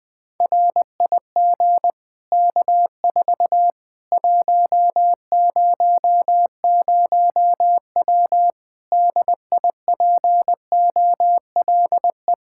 Morse code, heard as a keyed tone, about 20 wpm, 700 Hz. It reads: RIG K4 100W DIPOLE
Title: Practice QSO - Exchange 2 - Info - 20wpm